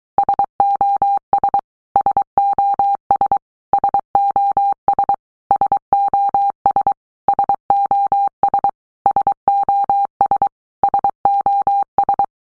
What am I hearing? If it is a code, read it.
Morse code, heard as a keyed tone, about 23 wpm, 800 Hz. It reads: SOS SOS SOS SOS SOS SOS SOS
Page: SOS sound in Morse Code (Fast)